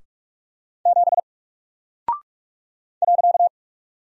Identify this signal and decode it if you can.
Morse code, heard as a keyed tone, about 45 wpm, 700 Hz. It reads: 7 .
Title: Single Character (Letter:Number:Punctuation:Prosign) Alt 45wpm